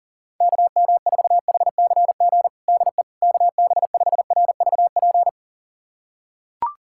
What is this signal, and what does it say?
Morse code, heard as a keyed tone, about 40 wpm, 700 Hz. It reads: KM4HCG DE KB5RVP
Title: USA Callsign DE Callsign 40wpm